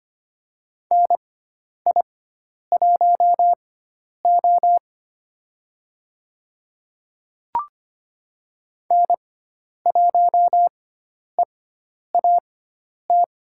Morse code, heard as a keyed tone, about 25 wpm, spaced as for 10 wpm, 700 Hz. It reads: NI1O N1EAT
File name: Lesson 16g - Letter c - Call Signs 25wpm